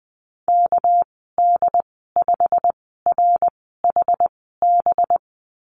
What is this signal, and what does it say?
Morse code, heard as a keyed tone, about 20 wpm, spaced as for 15 wpm, 700 Hz. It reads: KD5RHB
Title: 200 Lesson 32g - Letter k - Call Signs 20wpm